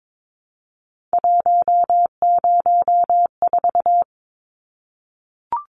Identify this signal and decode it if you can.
Morse code, heard as a keyed tone, about 22 wpm, 700 Hz. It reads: 104
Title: Numbers - 3-Digit 22wpm